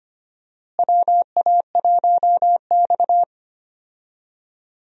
Morse code, heard as a keyed tone, about 25 wpm, 700 Hz. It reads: WA1X
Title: USA Call Signs - 4 Characters - Repeated 2x - 25wpm